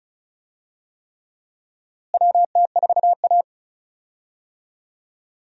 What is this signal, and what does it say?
Morse code, heard as a keyed tone, about 34 wpm, 700 Hz. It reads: WT4A
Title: USA Call Signs - 4 Characters - Repeated 3x - 35wpm